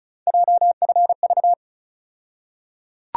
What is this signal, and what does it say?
Morse code, heard as a keyed tone, about 34 wpm, 700 Hz. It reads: JFV
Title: USA Call Sign Postfixes - 3 Letters 35wpm